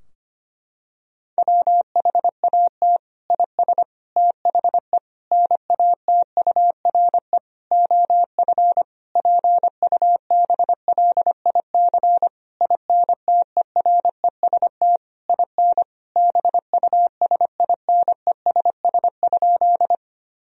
Morse code, heard as a keyed tone, about 25 wpm, 700 Hz. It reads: WHAT IS THE NATURE OF PUBLIC INTEREST IN BUSINESS?
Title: Sentences from Top 1000 Words - Review Part 3 - 25wpm